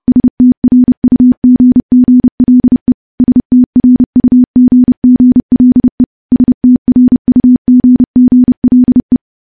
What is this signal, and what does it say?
Morse code, heard as a keyed tone, about 30 wpm, 250 Hz. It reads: STRUGGLE STRUGGLE STRUGGLE
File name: morse / morse-training-30WPM